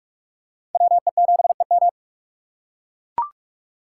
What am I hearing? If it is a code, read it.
Morse code, heard as a keyed tone, about 43 wpm, 700 Hz. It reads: WE7EM
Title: USA Call Signs - 5 Characters - 6x Character Spacing - 45wpm